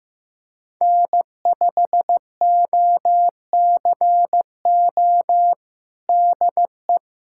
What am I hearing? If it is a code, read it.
Morse code, heard as a keyed tone, about 15 wpm, 700 Hz. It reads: N5OCO DE
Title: USA Callsign DE Callsign 15wpm